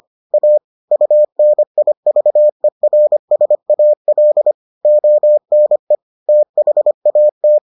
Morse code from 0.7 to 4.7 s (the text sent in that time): UNIVERSAL